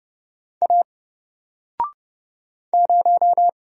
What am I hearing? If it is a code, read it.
Morse code, heard as a keyed tone, about 31 wpm, 700 Hz. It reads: A 0
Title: Single Letter-Number Alt 30wpm